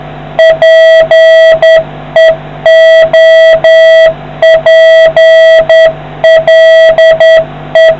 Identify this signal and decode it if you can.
Morse code, heard as a keyed tone, about 10 wpm, 650 Hz, with noise. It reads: PEOPLE